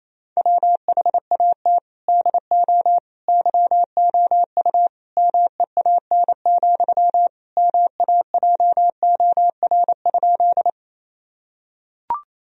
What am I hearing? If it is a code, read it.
Morse code, heard as a keyed tone, about 28 wpm, 700 Hz. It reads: WHAT DO YOU MEAN, MAJOR?
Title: Sentences from Top 1000 Words - Review Part 4 - 28wpm